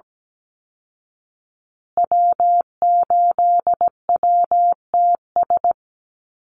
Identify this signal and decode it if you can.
Morse code, heard as a keyed tone, about 17 wpm, 700 Hz. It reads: W8WTS
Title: Call Signs Repeated 2x - 17wpm